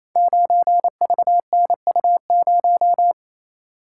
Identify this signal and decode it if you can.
Morse code, heard as a keyed tone, about 28 wpm, 690 Hz. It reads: 9VNU0
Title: Code Groups - 5 Letter:Numbers - 28wpm